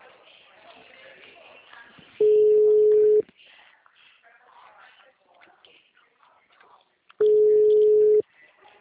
Ringback tone (Europe, 425 Hz).